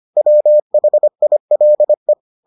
Morse code, heard as a keyed tone, about 25 wpm, 600 Hz. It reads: WHILE